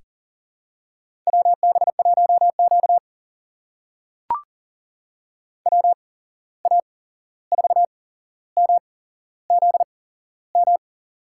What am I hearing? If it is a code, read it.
Morse code, heard as a keyed tone, about 40 wpm, spaced as for 15 wpm, 700 Hz. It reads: WB1Q WA4MZM
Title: USA Call Signs - 8x Character Spacing - 40wpm